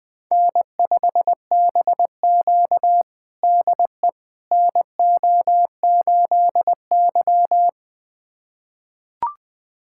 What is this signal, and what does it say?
Morse code, heard as a keyed tone, about 20 wpm, 700 Hz. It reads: N5BQ DE NO8Y
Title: USA Callsign DE Callsign - 4 Characters - 20wpm